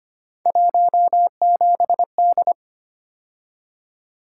Morse code, heard as a keyed tone, about 25 wpm, 700 Hz. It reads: 17D
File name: Three Letter-Number Combinations - 25wpm